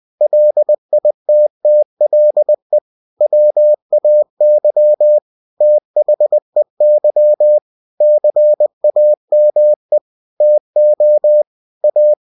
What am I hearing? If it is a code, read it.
Morse code, heard as a keyed tone, about 20 wpm, 600 Hz. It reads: LITTLE WAY THEY CAME TO A
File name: cw-books / winnie-the-pooh / winnie-20wpm-page-0083